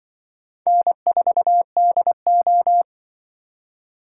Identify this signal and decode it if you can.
Morse code, heard as a keyed tone, about 24 wpm, 700 Hz. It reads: N4DO
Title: Speed Racing - USA 1x2 Call Signs 20wpm